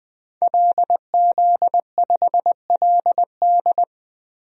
Morse code, heard as a keyed tone, about 20 wpm, 700 Hz. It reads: LZ5LD
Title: DXCC-100 Call Signs - 5 Characters - Repeated 2x - 20wpm